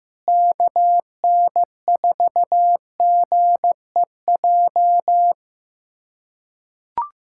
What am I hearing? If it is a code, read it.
Morse code, heard as a keyed tone, about 15 wpm, 700 Hz. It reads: KN4GEJ
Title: USA Callsign DE Callsign 15wpm